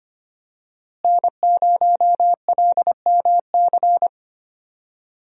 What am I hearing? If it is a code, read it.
Morse code, heard as a keyed tone, about 25 wpm, 700 Hz. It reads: N0LMC
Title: USA Call Signs - 5 Characters - Repeated 3x - 25wpm